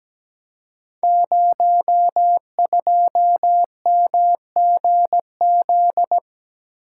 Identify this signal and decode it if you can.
Morse code, heard as a keyed tone, about 17 wpm, 700 Hz. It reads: 02MGZ
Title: Code Groups - 5 Letter:Numbers - 17wpm